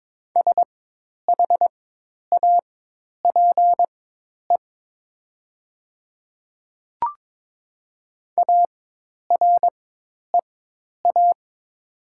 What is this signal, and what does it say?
Morse code, heard as a keyed tone, about 22 wpm, spaced as for 10 wpm, 700 Hz. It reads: SHAPE AREA
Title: 4x Character Spacing - Top 500 Words - 22wpm